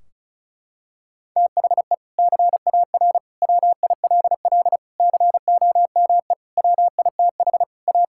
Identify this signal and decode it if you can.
Morse code, heard as a keyed tone, about 36 wpm, 700 Hz. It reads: THE CAR WILL COME WITH A
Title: Sentences from Top 1600 Words 35wpm